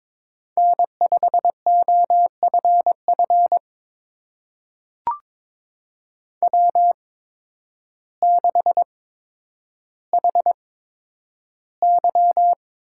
Morse code, heard as a keyed tone, about 22 wpm, spaced as for 10 wpm, 700 Hz. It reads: N5OFF W6HY
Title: USA Call Signs - 8x Character Spacing - 22wpm